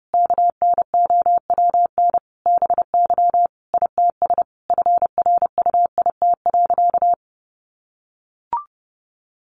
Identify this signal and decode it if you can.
Morse code, heard as a keyed tone, about 30 wpm, 700 Hz. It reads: KNOWN BY ITS FRUIT.
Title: Sentences from Top 1300 Words 30wpm